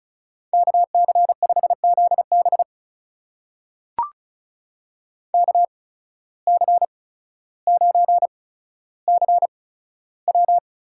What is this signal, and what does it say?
Morse code, heard as a keyed tone, about 34 wpm, spaced as for 15 wpm, 700 Hz. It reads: KC5ZB KC9CW
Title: USA Call Signs - 5 Characters - 8x Character Spacing - 35wpm